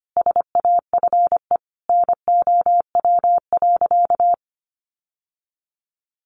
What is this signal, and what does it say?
Morse code, heard as a keyed tone, about 25 wpm, 700 Hz. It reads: SAFE NOW.
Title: Sentences from Top 900 Words 25wpm